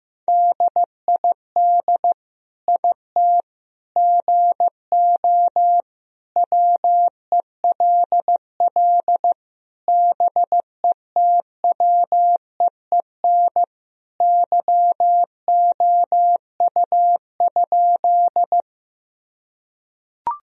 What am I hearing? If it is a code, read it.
Morse code, heard as a keyed tone, about 15 wpm, 700 Hz. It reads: DID IT GO WELL BETWEEN YOU?
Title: Sentences from Top 1000 Words - Review Part 1 - 15wpm